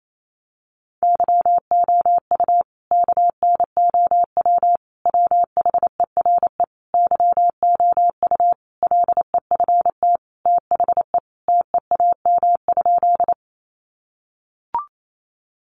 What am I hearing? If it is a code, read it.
Morse code, heard as a keyed tone, about 28 wpm, 700 Hz. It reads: YOU KNOW WHERE YOU LEFT THE TEAM?
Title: Sentences from Top 1500 Words 28wpm